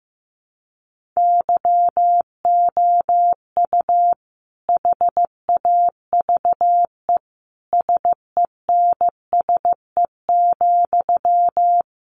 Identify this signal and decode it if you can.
Morse code, heard as a keyed tone, about 15 wpm, 700 Hz. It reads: YOU HAVE SENSE,
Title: Sentences from Top 1100 Words 15wpm